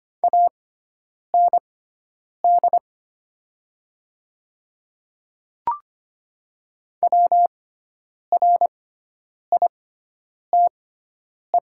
Morse code, heard as a keyed tone, about 25 wpm, spaced as for 9 wpm, 700 Hz. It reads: AND WRITE
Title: 6x Character Spacing - Top 100 Words - 25wpm